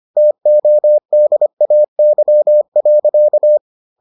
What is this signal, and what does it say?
Morse code, heard as a keyed tone, about 25 wpm, 600 Hz. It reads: TODAY.